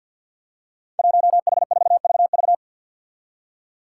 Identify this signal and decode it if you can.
Morse code, heard as a keyed tone, about 48 wpm, 700 Hz. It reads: 1H4VV
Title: Five Challenging Letter-Number Combinations 50wpm